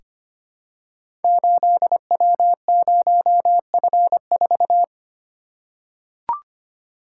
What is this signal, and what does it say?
Morse code, heard as a keyed tone, about 25 wpm, 700 Hz. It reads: 8W0F4
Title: Code Groups - 5 Letter:Numbers - 25wpm